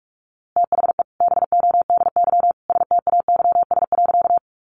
Morse code, heard as a keyed tone, about 44 wpm, 700 Hz. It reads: THE BODY STAYS.
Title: A Sound Of Thunder 45wpm